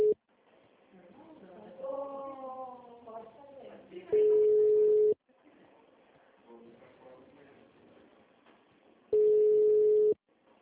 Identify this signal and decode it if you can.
Ringback tone (Europe, 425 Hz), with speech.
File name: out- -2007- - - .156